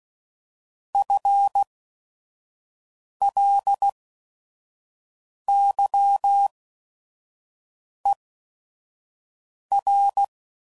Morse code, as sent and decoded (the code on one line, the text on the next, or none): ..-. .-.. -.-- . .-.
FLYER